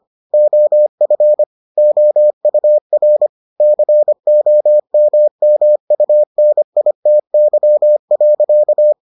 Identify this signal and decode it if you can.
Morse code, heard as a keyed tone, about 25 wpm, 600 Hz. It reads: OF OUR COMMUNITY.